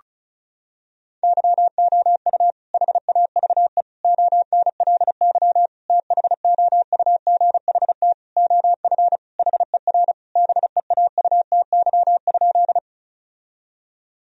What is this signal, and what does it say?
Morse code, heard as a keyed tone, about 36 wpm, 700 Hz. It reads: YOU HAVE ONLY THOUGHT OF HER BEAUTY?
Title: Sentences from Top 1000 Words - Review Part 2 - 35wpm